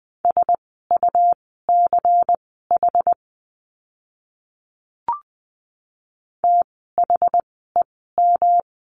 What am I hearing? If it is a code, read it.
Morse code, heard as a keyed tone, about 20 wpm, spaced as for 14 wpm, 700 Hz. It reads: SUCH THEM
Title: 192 Lesson 31f - Letter v - Words 20wpm